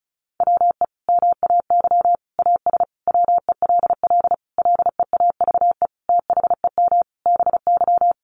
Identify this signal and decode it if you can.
Morse code, heard as a keyed tone, about 34 wpm, 700 Hz. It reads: WE MAY AS WELL LEAVE THEM BY
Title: Sentences from Top 800 Words 35wpm